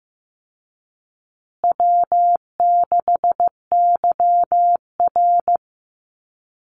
Morse code, heard as a keyed tone, about 15 wpm, 700 Hz. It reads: W6YR
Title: Easy Call Signs Repeated 2x - 15wpm